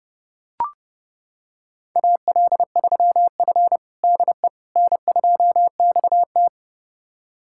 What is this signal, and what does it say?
Morse code, heard as a keyed tone, about 30 wpm, 700 Hz. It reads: AL3F DE N2XT
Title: USA Callsign DE Callsign - 4 Characters - 30wpm